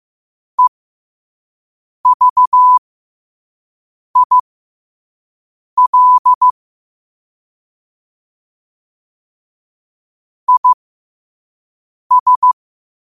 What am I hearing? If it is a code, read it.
Morse code, heard as a keyed tone, about 15 wpm, spaced as for 5 wpm, 1000 Hz. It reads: EVIL IS
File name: 5wpm